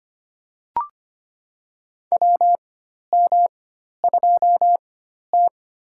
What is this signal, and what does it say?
Morse code, heard as a keyed tone, about 25 wpm, spaced as for 12 wpm, 700 Hz. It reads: WM2T
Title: USA Call Signs - 4 Characters - 4x Character Spacing - 25wpm